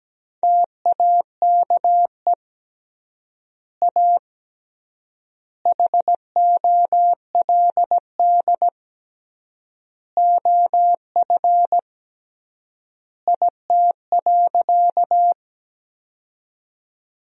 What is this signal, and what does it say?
Morse code, heard as a keyed tone, about 17 wpm, 700 Hz. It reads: TAKE A HOLD OF IT.
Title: 3x Word-Spacing - Sentences from Top 400 Words - 17wpm